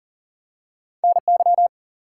Morse code, heard as a keyed tone, about 38 wpm, 700 Hz. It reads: NY
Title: US States and Canadian Provinces 40wpm